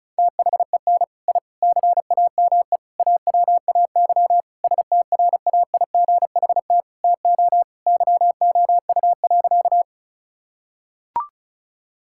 Morse code, heard as a keyed tone, about 36 wpm, 690 Hz. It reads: THEN I CAME AWAY STRAIGHT TO YOU.
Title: Sentences from Top 1000 Words - Review Part 6 - 35wpm